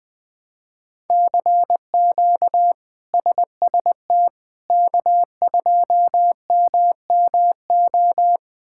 Morse code, heard as a keyed tone, about 20 wpm, 700 Hz. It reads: CQ SST K2MMO
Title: K1USN SST QSO Practice - 20wpm